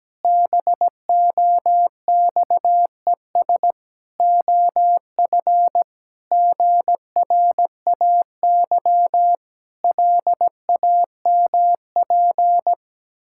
Morse code, heard as a keyed tone, about 17 wpm, 700 Hz. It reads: BOXES OF GRAY LAMP
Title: Pangram Challenge v2 - 17wpm